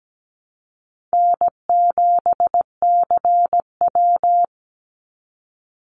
Morse code, heard as a keyed tone, about 17 wpm, 700 Hz. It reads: N7CW
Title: NA SOTA Activator Callsigns 17wpm